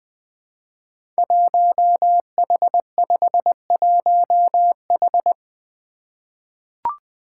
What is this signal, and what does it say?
Morse code, heard as a keyed tone, about 20 wpm, 700 Hz. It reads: 1H51H
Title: Five Challenging Letter-Number Combinations 20wpm